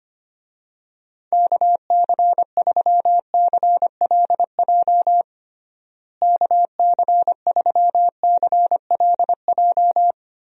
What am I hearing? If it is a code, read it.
Morse code, heard as a keyed tone, about 25 wpm, 700 Hz. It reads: KC3CLJ KC3CLJ
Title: USA Call Signs - 6 Characters - Repeated 3x - 25wpm